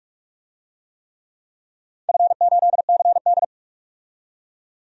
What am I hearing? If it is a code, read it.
Morse code, heard as a keyed tone, about 47 wpm, 700 Hz. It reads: F8CD
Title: DXCC-100 Call Signs - 4 Characters - 45wpm